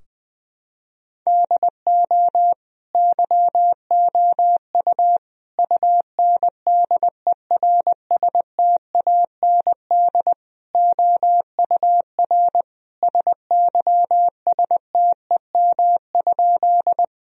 Morse code, heard as a keyed tone, about 20 wpm, 700 Hz. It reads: DO YOU UNDERSTAND OUR SYSTEM?
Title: Sentences from Top 1000 Words - Review Part 3 - 20wpm